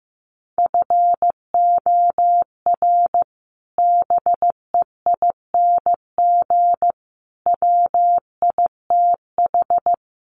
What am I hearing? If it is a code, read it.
Morse code, heard as a keyed tone, about 15 wpm, 700 Hz. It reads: FOR BEING WITH
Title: Sets of 4 Words 15wpm